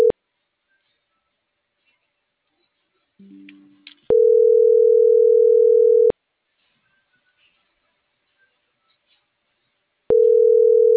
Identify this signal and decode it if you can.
Ringback tone (North America), 440 + 480 Hz.